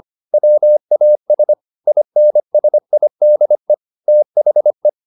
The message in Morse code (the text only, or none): WAS INSIDE THE